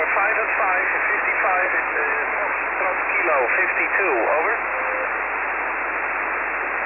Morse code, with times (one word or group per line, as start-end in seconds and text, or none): none